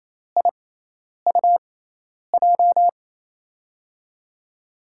Morse code, heard as a keyed tone, about 27 wpm, spaced as for 10 wpm, 700 Hz. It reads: IUJ
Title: USA Call Signs - 6 Characters - 6x Character Spacing - 28wpm